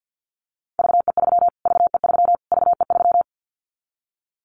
Morse code, heard as a keyed tone, about 50 wpm, 700 Hz. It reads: VE3 VE3 VE3
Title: Callsign Prefixes for Top 25 DXCC - Repeated 3x - 50wpm